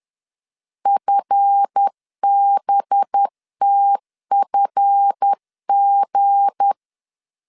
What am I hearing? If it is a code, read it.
Morse code, heard as a keyed tone, about 10 wpm, 800 Hz. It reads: FBTFG
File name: LEZIONI / 7 GF9Z / 7 GF 10wpm 2514